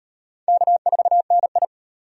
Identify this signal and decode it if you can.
Morse code, heard as a keyed tone, about 34 wpm, 700 Hz. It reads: K4NI